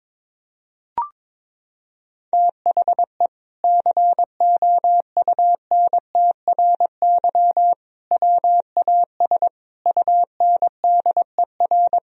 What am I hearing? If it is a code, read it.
Morse code, heard as a keyed tone, about 22 wpm, 700 Hz. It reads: THE COUNTRY WAS UNDER
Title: Sentences from Top 1000 Words - Review Part 1 - 22wpm